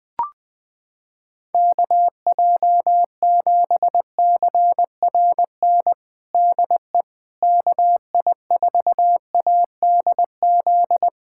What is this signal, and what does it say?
Morse code, heard as a keyed tone, about 21 wpm, 700 Hz. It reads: KJ7CRN DE KI4ADZ
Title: USA Callsign DE Callsign - 6 Characters - 20wpm